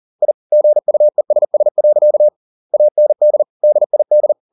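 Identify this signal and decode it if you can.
Morse code, heard as a keyed tone, about 40 wpm, 600 Hz. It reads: I GUESS. AND DID
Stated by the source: Morse code at 40 wpm, practice text, CW med Gnister.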